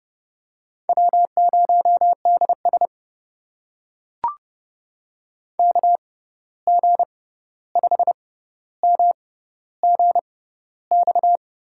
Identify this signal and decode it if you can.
Morse code, heard as a keyed tone, about 30 wpm, spaced as for 14 wpm, 700 Hz. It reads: W0DS KG5MGX
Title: USA Call Signs - 6x Character Spacing - 30wpm